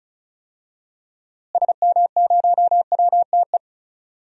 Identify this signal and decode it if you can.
Morse code, heard as a keyed tone, about 34 wpm, 700 Hz. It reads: SM0WTE
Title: DXCC-100 Call Signs - 6 Characters - Repeated 2x - 35wpm